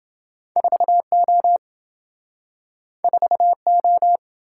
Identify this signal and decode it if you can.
Morse code, heard as a keyed tone, about 26 wpm, 700 Hz. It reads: 4O 4O